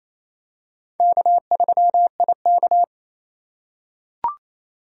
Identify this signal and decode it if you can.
Morse code, heard as a keyed tone, about 28 wpm, 700 Hz. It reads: K3IK
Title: USA Call Signs - 4 Characters - 3x Character Spacing - 28wpm